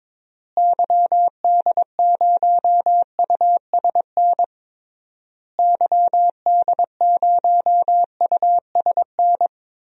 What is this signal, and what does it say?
Morse code, heard as a keyed tone, about 22 wpm, 700 Hz. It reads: YD0USN YD0USN
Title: DXCC-100 Call Signs - 6 Characters - Repeated 3x - 22wpm